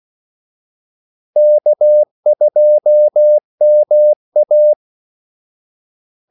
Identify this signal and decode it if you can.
Morse code, heard as a keyed tone, about 16 wpm, 600 Hz. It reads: K2MA
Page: Kaldesignaler 16 wpm